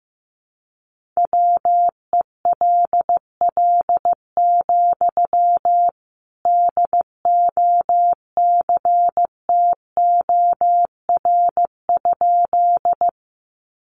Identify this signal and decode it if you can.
Morse code, heard as a keyed tone, about 15 wpm, 700 Hz. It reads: WELL, DOCTOR?
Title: Sentences from Top 1500 Words 15wpm